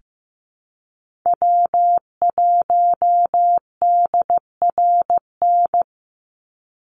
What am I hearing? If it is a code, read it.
Morse code, heard as a keyed tone, about 15 wpm, 700 Hz. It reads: W1DRN
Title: USA Call Signs 15wpm